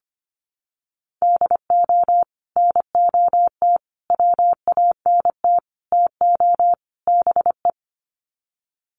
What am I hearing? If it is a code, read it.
Morse code, heard as a keyed tone, about 25 wpm, 700 Hz. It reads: DO NOT WANT TO BE
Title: Sets of 5 Words 25wpm